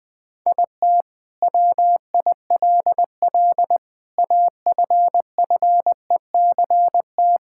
Morse code, heard as a keyed tone, about 20 wpm, 700 Hz. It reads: IT WILL AFFECT
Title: Sentences from Top 1000 Words - Review Part 6 - 20wpm